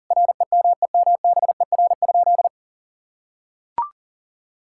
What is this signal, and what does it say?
Morse code, heard as a keyed tone, about 40 wpm, 700 Hz. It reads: REMEMBER?
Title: Sentences from Top 1000 Words - Review Part 6 - 40wpm